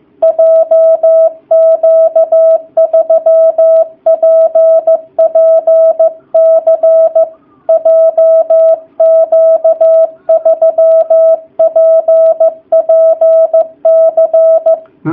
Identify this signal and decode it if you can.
Morse code, heard as a keyed tone, about 15 wpm, 660 Hz. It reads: JQ3PPC JQ3PPC
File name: jh8cbh / 2019cw / sp15- w